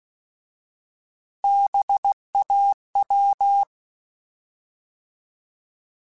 Morse code, heard as a keyed tone, about 16 wpm, 800 Hz. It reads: BAW
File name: Alphabetical / Alphabetical mp3s / Bs @15